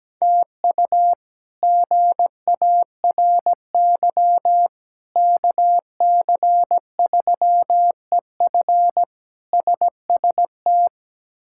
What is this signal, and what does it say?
Morse code, heard as a keyed tone, about 17 wpm, 700 Hz. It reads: TU GARY KC3EF SST
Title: K1USN SST QSO Practice - 17wpm